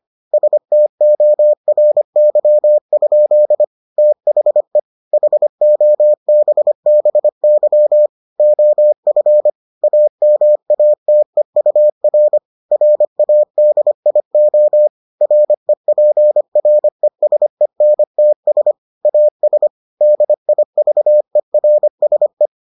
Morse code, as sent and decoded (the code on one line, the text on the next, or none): ... - --- .-. -.-- ..--.. / - .... / .... --- -... -... -.-- / --- ..-. / .- -- .- - . ..- .-. / .-. .- -.. .. --- / .-. .--. .-. ... -. - ... / .- ... / -.. .. ...- . .-. ... .
STORY? THE HOBBY OF AMATEUR RADIO REPRESENTS AS DIVERSE